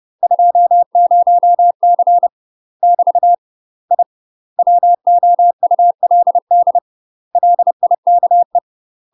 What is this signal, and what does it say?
Morse code, heard as a keyed tone, about 30 wpm, 700 Hz. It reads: 20C <BT> I WOULD LIKE
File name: UK MASTER QSOs TRAINING / 30WPM / UK MASTER QSOs 0015